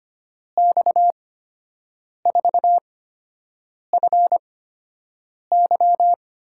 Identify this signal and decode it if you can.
Morse code, heard as a keyed tone, about 25 wpm, spaced as for 7 wpm, 700 Hz. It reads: X4FY
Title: USA Call Signs - 8x Character Spacing - 25wpm